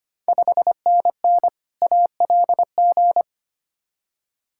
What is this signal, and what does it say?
Morse code, heard as a keyed tone, about 25 wpm, 700 Hz. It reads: 5NN ALG
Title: Ontario QSO Party Exchange 25wpm